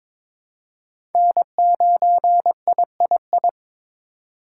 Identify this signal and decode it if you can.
Morse code, heard as a keyed tone, about 21 wpm, 700 Hz. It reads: N9III
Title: USA Call Signs - 5 Characters - 3x Character Spacing - 22wpm